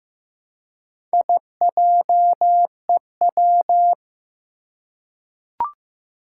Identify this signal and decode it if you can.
Morse code, heard as a keyed tone, about 15 wpm, 700 Hz. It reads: IJEW
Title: Four Challenging Letter-Number Combinations 15wpm